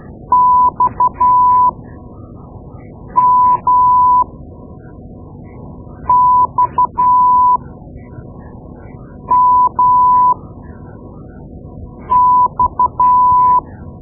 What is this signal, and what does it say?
Morse code, heard as a keyed tone, about 12 wpm, spaced as for 4 wpm, 1000 Hz. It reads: XMXMX
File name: Morse Lessons / Morse Lessons / Tape-2B